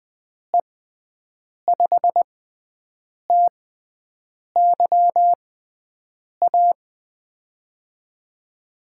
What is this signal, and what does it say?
Morse code, heard as a keyed tone, about 20 wpm, spaced as for 7 wpm, 700 Hz. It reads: E5TYA
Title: USA Call Signs - 6 Characters - 6x Character Spacing - 20wpm